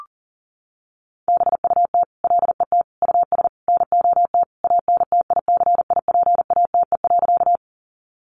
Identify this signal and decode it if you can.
Morse code, heard as a keyed tone, about 38 wpm, 700 Hz. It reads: BUT LET US NOT ANTICIPATE.
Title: Sentences from Top 1500 Words 40wpm